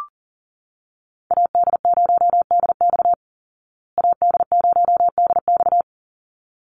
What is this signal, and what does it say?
Morse code, heard as a keyed tone, about 40 wpm, 700 Hz. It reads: AD0DX AD0DX
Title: Call Signs Repeated 3x - 40wpm